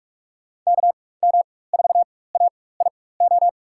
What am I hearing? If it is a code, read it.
Morse code, heard as a keyed tone, about 43 wpm, spaced as for 21 wpm, 700 Hz. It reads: KM4AIO
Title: USA Call Signs - 4x Character Spacing - 45wpm